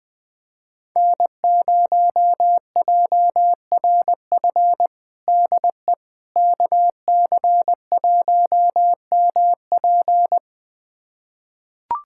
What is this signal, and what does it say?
Morse code, heard as a keyed tone, about 20 wpm, 700 Hz. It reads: N0JRF DE KC1MP
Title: USA Callsign DE Callsign - 5 Characters - 20wpm